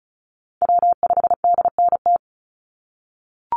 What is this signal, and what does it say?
Morse code, heard as a keyed tone, about 36 wpm, 700 Hz. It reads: W5DNT
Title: Call Signs - 35wpm